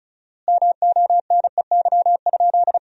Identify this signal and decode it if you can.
Morse code, heard as a keyed tone, about 36 wpm, 700 Hz. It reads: MONEY?
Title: 4x Word-Spacing - Sentences from Top 400 Words - 35wpm